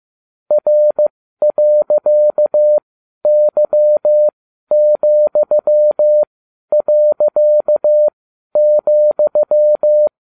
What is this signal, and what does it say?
Morse code, heard as a keyed tone, about 15 wpm, spaced as for 11 wpm, 600 Hz. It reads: R.Y,.,
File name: cw / ubacw20 2